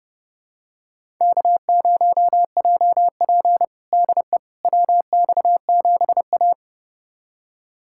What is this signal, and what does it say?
Morse code, heard as a keyed tone, about 31 wpm, 700 Hz. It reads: K0JP DE WX7A
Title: USA Callsign DE Callsign - 4 Characters - 30wpm